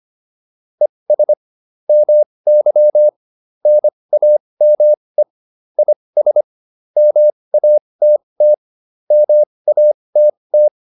Morse code, heard as a keyed tone, about 25 wpm, spaced as for 20 wpm, 600 Hz. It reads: ES MY NAME IS MATT MATT
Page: intermediate-practice-files